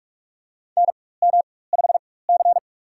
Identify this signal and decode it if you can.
Morse code, heard as a keyed tone, about 43 wpm, spaced as for 21 wpm, 700 Hz. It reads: NM5C
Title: USA Call Signs - 4 Characters - 4x Character Spacing - 45wpm